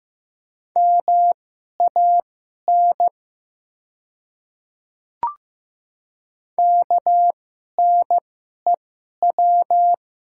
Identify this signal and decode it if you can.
Morse code, heard as a keyed tone, about 15 wpm, spaced as for 11 wpm, 700 Hz. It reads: MAN KNEW
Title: 2x Character Spacing - Top 500 Words - 15wpm